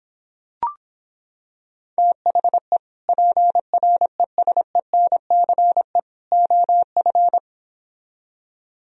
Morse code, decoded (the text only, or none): THE PRESENCE OF